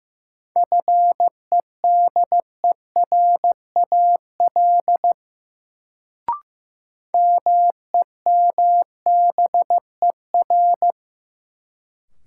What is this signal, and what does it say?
Morse code, heard as a keyed tone, about 15 wpm, 700 Hz. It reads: FEDERAL MEMBER
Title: Sentences from Top 1100 Words - New Word Drill Alt - 15wpm